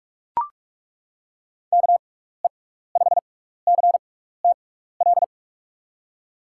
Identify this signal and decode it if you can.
Morse code, heard as a keyed tone, about 42 wpm, spaced as for 15 wpm, 700 Hz. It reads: KE5CTL
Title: USA Call Signs - 6x Character Spacing - 45wpm